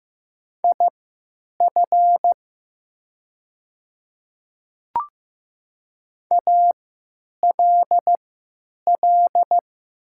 Morse code, heard as a keyed tone, about 15 wpm, spaced as for 9 wpm, 700 Hz. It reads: IF ALL
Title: 3x Character Spacing - Top 100 Words - 15wpm